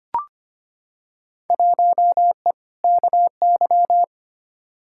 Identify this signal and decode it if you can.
Morse code, heard as a keyed tone, about 25 wpm, 700 Hz. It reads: 1E KY
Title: Field Day Exchanges - 25wpm